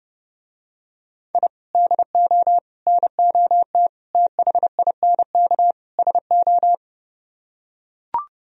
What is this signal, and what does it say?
Morse code, heard as a keyed tone, about 30 wpm, 700 Hz. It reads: I DO NOT THINK SO